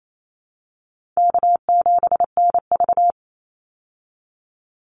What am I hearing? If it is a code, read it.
Morse code, heard as a keyed tone, about 28 wpm, 700 Hz. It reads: K7NV
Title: Easy Call Signs Repeated 3x - 28wpm